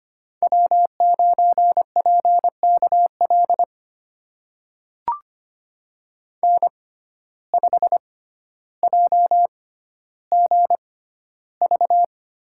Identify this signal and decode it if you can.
Morse code, heard as a keyed tone, about 25 wpm, spaced as for 13 wpm, 700 Hz. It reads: W9PKL N5JGV
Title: USA Call Signs - 6x Character Spacing - 25wpm